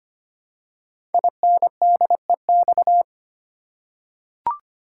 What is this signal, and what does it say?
Morse code, heard as a keyed tone, about 25 wpm, 700 Hz. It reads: INDEX
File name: 5 Letter Words - First 3 Letters Given - 25wpm